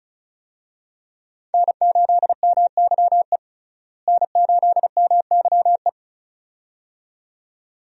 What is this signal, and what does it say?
Morse code, heard as a keyed tone, about 34 wpm, 700 Hz. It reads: N8MYE N8MYE
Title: USA Call Signs - 5 Characters - Repeated 2x - 35wpm